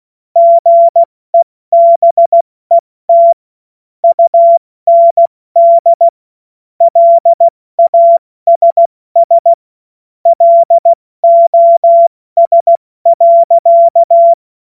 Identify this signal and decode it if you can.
Morse code, heard as a keyed tone, about 16 wpm, 680 Hz. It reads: GEBET UND LASS LOS.
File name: Morsekurs / Witze / Witz 11. Januar 2022, 70 F80